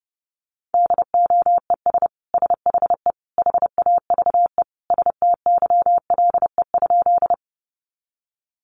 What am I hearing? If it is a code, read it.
Morse code, heard as a keyed tone, about 30 wpm, 700 Hz. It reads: DOES SHE HAVE STYLE?
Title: Sentences from Top 800 Words 30wpm